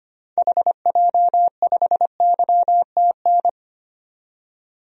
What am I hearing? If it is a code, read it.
Morse code, heard as a keyed tone, about 25 wpm, 700 Hz. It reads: HJ5YTN
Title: DXCC-100 Call Signs - 6 Characters - 25wpm